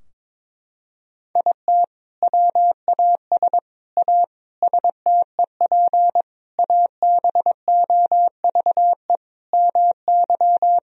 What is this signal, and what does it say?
Morse code, heard as a keyed tone, about 22 wpm, 700 Hz. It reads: IT WAS A STEP ABOVE MY
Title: Sentences from Top 1000 Words - Review Part 3 - 22wpm